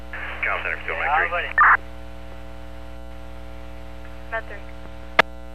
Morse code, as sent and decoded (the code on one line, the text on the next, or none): none